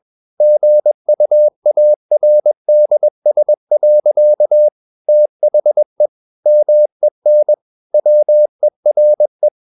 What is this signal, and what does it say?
Morse code, heard as a keyed tone, about 21 wpm, 600 Hz. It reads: GUARDS. THE MEN WERE
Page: CW med Gnister